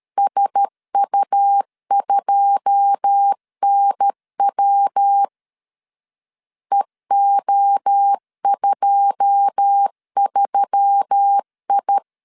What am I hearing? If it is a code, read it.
Morse code, heard as a keyed tone, about 13 wpm, 800 Hz. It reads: SU2NW EO23I
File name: LEZIONI / 9 2 12wpm 2301